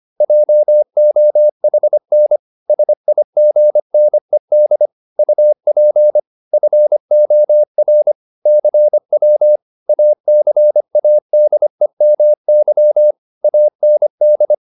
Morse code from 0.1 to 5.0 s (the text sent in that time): JOHN SIGNED